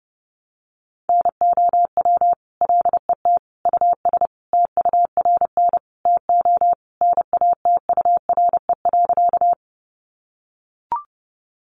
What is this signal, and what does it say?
Morse code, heard as a keyed tone, about 30 wpm, 700 Hz. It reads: NOW LET US TURN TO NATURE.
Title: Sentences from Top 600 Words 30wpm